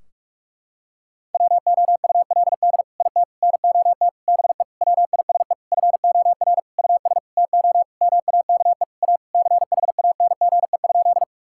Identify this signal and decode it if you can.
Morse code, heard as a keyed tone, about 43 wpm, 700 Hz. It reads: WOULD IT NOT BE WISE FOR US TO MAKE A CHANGE?
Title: Sentences from Top 1600 Words 45wpm